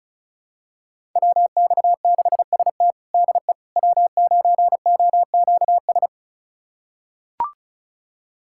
Morse code, heard as a keyed tone, about 36 wpm, 700 Hz. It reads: WX6ST DE W9OQS
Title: USA Callsign DE Callsign - 5 Characters - 35wpm